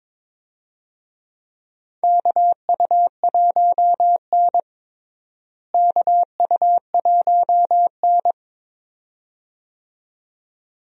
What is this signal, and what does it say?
Morse code, heard as a keyed tone, about 22 wpm, 700 Hz. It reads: KU1N KU1N
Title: USA Call Signs - 4 Characters - Repeated 2x - 22wpm